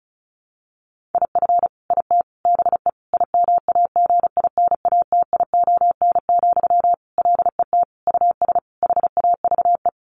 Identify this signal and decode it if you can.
Morse code, heard as a keyed tone, about 34 wpm, 700 Hz. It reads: IF IT BE IMAGINATION, LET US HAVE
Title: Sentences from Top 1300 Words 35wpm